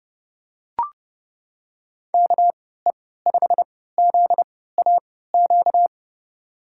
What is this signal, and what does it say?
Morse code, heard as a keyed tone, about 31 wpm, spaced as for 17 wpm, 700 Hz. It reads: KE5ZAQ
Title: USA Call Signs - 6 Characters - 3x Character Spacing - 30wpm